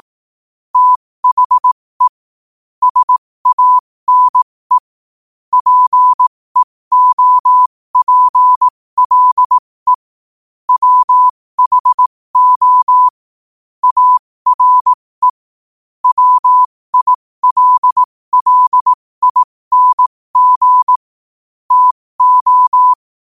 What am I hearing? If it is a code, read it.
Morse code, heard as a keyed tone, about 18 wpm, spaced as for 15 wpm, 1000 Hz. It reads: THE SANE PEOPLE WHO ARE WILLING TO
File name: qotdfiles / 15wpm